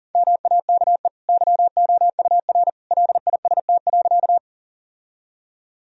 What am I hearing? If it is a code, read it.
Morse code, heard as a keyed tone, about 40 wpm, 700 Hz. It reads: MAKE YOUR LIST.
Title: Sentences from Top 1000 Words - Review Part 2 - 40wpm